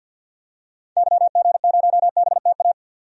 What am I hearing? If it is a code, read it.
Morse code, heard as a keyed tone, about 50 wpm, 700 Hz. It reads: YG0BTA
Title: DXCC-100 Call Signs - 6 Characters - 50wpm